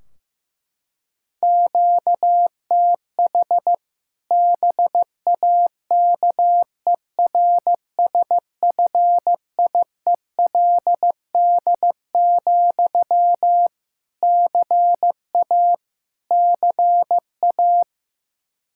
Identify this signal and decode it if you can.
Morse code, heard as a keyed tone, about 15 wpm, 700 Hz. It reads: QTH BAKERSFIELD, CA CA
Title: Practice QSO - Exchange 1 - Beginning - 15wpm